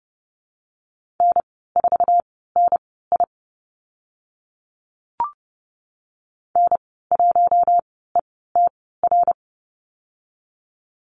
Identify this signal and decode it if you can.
Morse code, heard as a keyed tone, about 28 wpm, spaced as for 17 wpm, 700 Hz. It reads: N4NI N1ETR
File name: Lesson 15g - Number 5 - Call Signs 30wpm